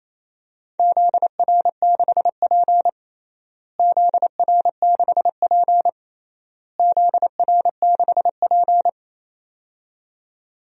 Morse code, heard as a keyed tone, about 28 wpm, 700 Hz. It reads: ZR6P ZR6P ZR6P
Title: DXCC-100 Call Signs - 4 Characters - Repeated 3x - 28wpm